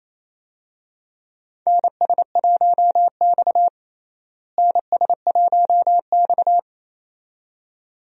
Morse code, heard as a keyed tone, about 28 wpm, 700 Hz. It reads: NS1X NS1X
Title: USA Call Signs - 4 Characters - Repeated 2x - 28wpm